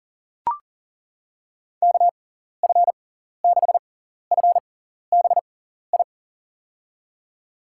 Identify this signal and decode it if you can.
Morse code, heard as a keyed tone, about 37 wpm, spaced as for 14 wpm, 700 Hz. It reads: KF6FBI
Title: USA Call Signs - 6x Character Spacing - 40wpm